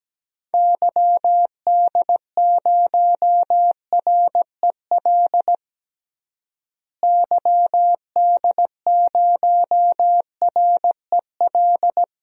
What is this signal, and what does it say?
Morse code, heard as a keyed tone, about 17 wpm, 700 Hz. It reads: YD0REL YD0REL
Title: DXCC-100 Call Signs - 6 Characters - Repeated 2x - 17wpm